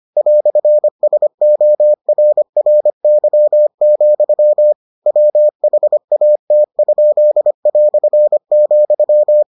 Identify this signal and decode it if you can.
Morse code, heard as a keyed tone, about 25 wpm, 600 Hz. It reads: "SORRY, WHAT?",